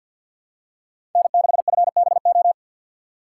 Morse code, heard as a keyed tone, about 51 wpm, 700 Hz. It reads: N6FBO
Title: USA Call Signs - 5 Characters - 2x Character Spacing - 50wpm